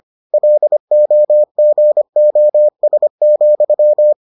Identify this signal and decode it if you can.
Morse code, heard as a keyed tone, about 25 wpm, 600 Hz. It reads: LOGOS,